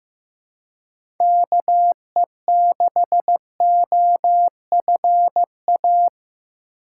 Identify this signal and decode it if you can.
Morse code, heard as a keyed tone, about 15 wpm, 700 Hz. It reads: KE6OFA
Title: USA Call Signs - 6 Characters - 4x Character Spacing - 15wpm